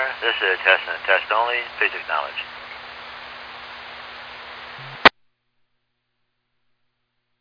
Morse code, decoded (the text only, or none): none